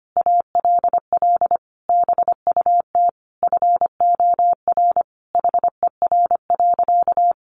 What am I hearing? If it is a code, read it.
Morse code, heard as a keyed tone, about 25 wpm, 700 Hz. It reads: ALL BUT FOR HER.
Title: Sentences from Top 1400 Words 25wpm